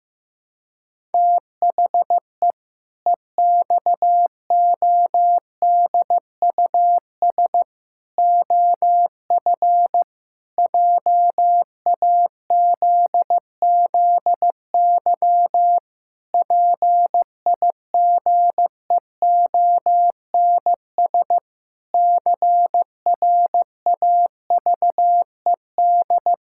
Morse code, heard as a keyed tone, about 16 wpm, 700 Hz. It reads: THE EXODUS OF JAZZY PIGEONS CRAVED
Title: Pangram Challenge v2 - 15wpm